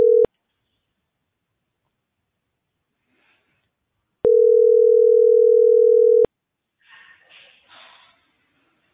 Ringback tone (North America), 440 + 480 Hz.